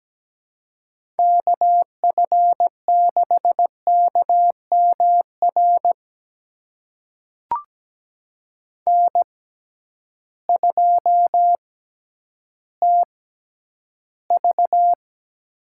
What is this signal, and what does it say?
Morse code, heard as a keyed tone, about 17 wpm, spaced as for 10 wpm, 700 Hz. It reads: KF6KMR N2TV
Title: USA Call Signs - 6x Character Spacing - 17wpm